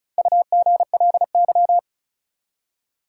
Morse code, heard as a keyed tone, about 34 wpm, 700 Hz. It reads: UGLY
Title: Four Letter Words - 35wpm